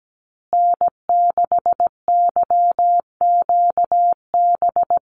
Morse code, heard as a keyed tone, about 17 wpm, 700 Hz. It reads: N6YQB
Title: USA Call Signs Repeated 2x 17wpm